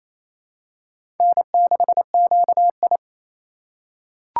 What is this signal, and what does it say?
Morse code, heard as a keyed tone, about 28 wpm, 700 Hz. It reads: N6QI
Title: USA Call Signs - 4 Characters - 6x Character Spacing - 28wpm